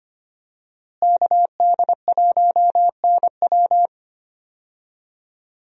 Morse code, heard as a keyed tone, about 25 wpm, 700 Hz. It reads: KD1NW
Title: USA Call Signs - 5 Characters - Repeated 3x - 25wpm